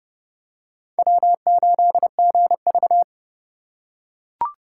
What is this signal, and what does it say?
Morse code, heard as a keyed tone, about 26 wpm, 700 Hz. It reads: W8GV